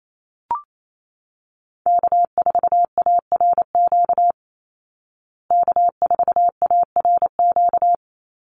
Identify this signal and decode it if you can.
Morse code, heard as a keyed tone, about 28 wpm, 690 Hz. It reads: K4ARQ K4ARQ
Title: Call Signs Repeated 3x - 28wpm